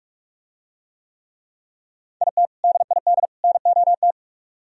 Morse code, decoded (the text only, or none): IT DID NOT